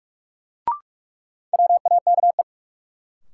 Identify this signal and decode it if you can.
Morse code, heard as a keyed tone, about 46 wpm, 700 Hz. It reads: WAKE
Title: Sets of 4 Words - New Word Drill - 45wpm